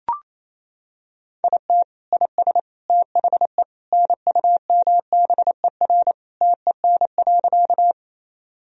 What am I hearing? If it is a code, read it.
Morse code, heard as a keyed tone, about 27 wpm, 700 Hz. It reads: IT IS THE NUMBER TEN.
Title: Sentences from Top 1000 Words - Review Part 2 - 28wpm